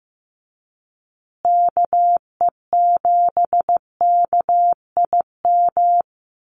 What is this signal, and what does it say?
Morse code, heard as a keyed tone, about 15 wpm, 700 Hz. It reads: KE7KIM
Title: USA Call Signs Repeated 3x 15wpm